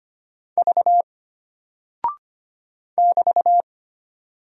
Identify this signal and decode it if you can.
Morse code, heard as a keyed tone, about 25 wpm, 700 Hz. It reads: V <BT>
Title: Single Character (Letter:Number:Punctuation:Prosign) Alt 25wpm